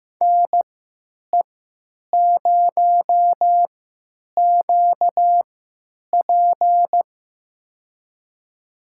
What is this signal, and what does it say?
Morse code, heard as a keyed tone, about 15 wpm, spaced as for 9 wpm, 700 Hz. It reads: NE0QP
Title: USA Call Signs - 5 Characters - 3x Character Spacing - 15wpm